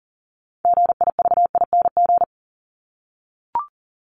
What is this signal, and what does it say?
Morse code, heard as a keyed tone, about 38 wpm, 700 Hz. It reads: GIVING
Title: ING Suffix - Top 100 Words with Base Words 40wpm